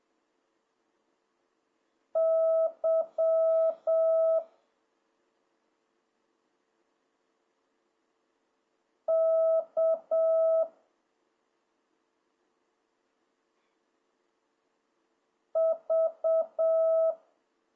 Morse code, heard as a keyed tone, about 7 wpm, 650 Hz. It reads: Y K V